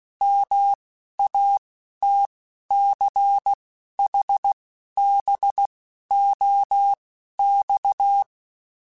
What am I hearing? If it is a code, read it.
Morse code, heard as a keyed tone, about 16 wpm, spaced as for 11 wpm, 800 Hz. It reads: MATCHBOX
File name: QSOs / QSOs-mp3s / WD0FBT 1@15